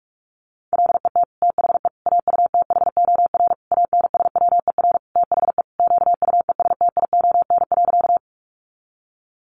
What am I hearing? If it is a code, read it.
Morse code, heard as a keyed tone, about 44 wpm, 700 Hz. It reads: LET THE AUTHOR ANSWER THE QUESTION.
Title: Sentences from Top 1200 Words 45wpm